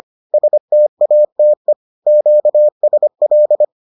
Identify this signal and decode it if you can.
Morse code, heard as a keyed tone, about 25 wpm, 600 Hz. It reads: STATE QSL